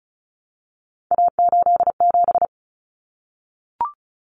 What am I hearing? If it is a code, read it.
Morse code, heard as a keyed tone, about 34 wpm, 700 Hz. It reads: A87
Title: Common Cut Numbers - 3-Digit 35wpm